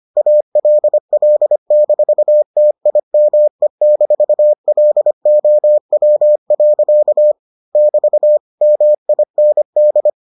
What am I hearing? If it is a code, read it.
Morse code, heard as a keyed tone, about 25 wpm, 600 Hz. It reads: ALL-TIME-LOW. <BT> MIND